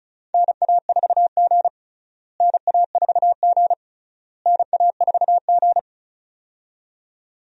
Morse code, heard as a keyed tone, about 36 wpm, 700 Hz. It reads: NA4G NA4G NA4G
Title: USA Call Signs - 4 Characters - Repeated 3x - 35wpm